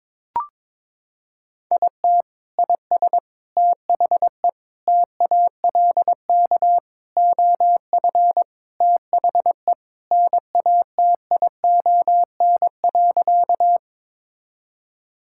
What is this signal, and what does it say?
Morse code, heard as a keyed tone, about 21 wpm, 700 Hz. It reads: IT IS THE TALK OF THE NATION.
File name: practice / 22wpm / Sentences with 4-Word Phrases - 22wpm